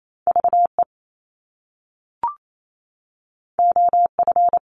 Morse code, heard as a keyed tone, about 28 wpm, 700 Hz. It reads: VE OF
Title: Top Two Letter Combinations in Words 28wpm